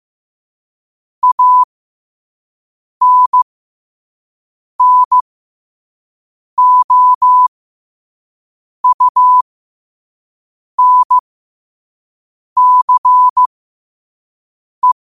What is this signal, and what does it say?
Morse code, heard as a keyed tone, about 15 wpm, spaced as for 5 wpm, 1000 Hz. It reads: ANNOUNCE
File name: qotdfiles / 5wpm